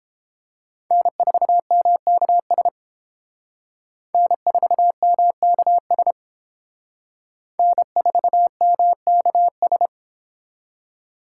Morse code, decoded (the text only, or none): N4MKS N4MKS N4MKS